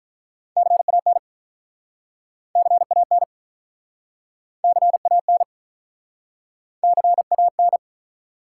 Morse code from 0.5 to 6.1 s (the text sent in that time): CAN CAN CAN